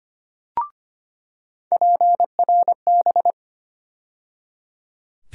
Morse code, heard as a keyed tone, about 25 wpm, 700 Hz. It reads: PRB
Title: USA Call Sign Postfixes - 3 Letters 25wpm